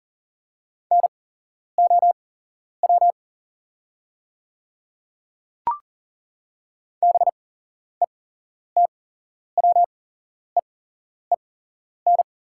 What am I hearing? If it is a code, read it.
Morse code, heard as a keyed tone, about 38 wpm, spaced as for 11 wpm, 690 Hz. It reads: NOW BETWEEN
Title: 8x Character Spacing - Top 300 Words - 40wpm